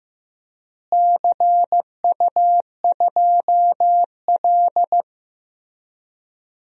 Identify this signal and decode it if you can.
Morse code, heard as a keyed tone, about 15 wpm, 700 Hz. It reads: CU2L
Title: DXCC-100 Call Signs - 4 Characters - Repeated 3x - 15wpm